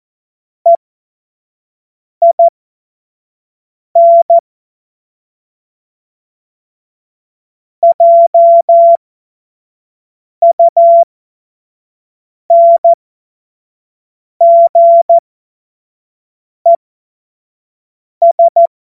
Morse code, heard as a keyed tone, about 14 wpm, spaced as for 5 wpm, 680 Hz. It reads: EIN JUNGES
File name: Morsekurs / Witze / Witz 20. Januar 2026 25 F70